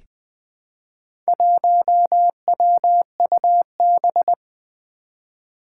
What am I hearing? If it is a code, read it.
Morse code, heard as a keyed tone, about 20 wpm, 700 Hz. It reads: 1WUB
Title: Four Challenging Letter-Number Combinations 20wpm